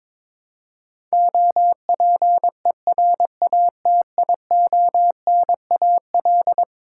Morse code, heard as a keyed tone, about 22 wpm, 700 Hz. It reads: OPERATIONAL
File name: practice / 22wpm / Eleven Letter Words - 22wpm